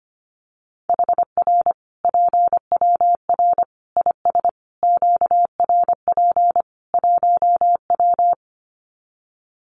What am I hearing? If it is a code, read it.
Morse code, heard as a keyed tone, about 25 wpm, 700 Hz. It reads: HR PWR IS QRP 1W
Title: Common QSO Phrases 25wpm